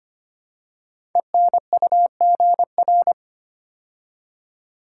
Morse code, heard as a keyed tone, about 25 wpm, 700 Hz. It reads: ENUGR
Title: Code Groups - 5 Letters - 25wpm